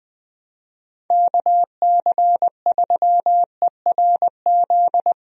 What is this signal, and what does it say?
Morse code, heard as a keyed tone, about 20 wpm, 700 Hz. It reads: KC3ERZ
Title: USA Call Signs - 6 Characters - Repeated 3x - 20wpm